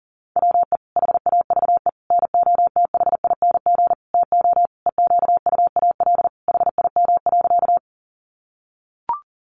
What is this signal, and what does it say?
Morse code, heard as a keyed tone, about 37 wpm, 700 Hz. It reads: WE HAVE NOTHING TO EQUAL HIM.
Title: Sentences from Top 1400 Words 40wpm